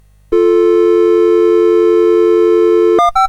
Dial tone (North America), 350 + 440 Hz.